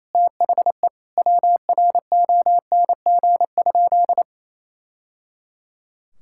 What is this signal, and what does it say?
Morse code, heard as a keyed tone, about 28 wpm, 700 Hz. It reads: THE WRONG?
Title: Sentences from Top 1000 Words - Review Part 6 - 28wpm